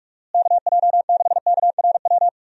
Morse code, heard as a keyed tone, about 46 wpm, 700 Hz. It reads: KJ6KRW
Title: USA Call Signs - 6 Characters - Repeated 3x - 45wpm